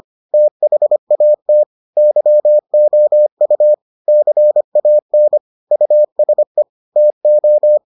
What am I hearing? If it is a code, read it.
Morse code, heard as a keyed tone, about 25 wpm, 600 Hz. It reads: THAT YOU CAN USE TO